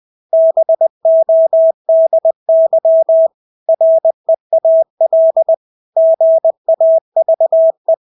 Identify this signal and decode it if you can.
Morse code, heard as a keyed tone, about 20 wpm, 650 Hz. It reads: BODY REAL GAVE
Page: Engelske ord 20 wpm